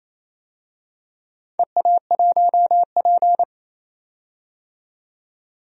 Morse code, heard as a keyed tone, about 28 wpm, 700 Hz. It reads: EA1P
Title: DXCC-100 Call Signs - 4 Characters - 28wpm